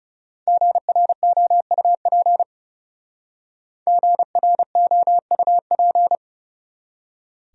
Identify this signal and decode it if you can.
Morse code, heard as a keyed tone, about 37 wpm, 700 Hz. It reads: GROUP GROUP
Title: Speed-Racing Top 300 Words 30wpm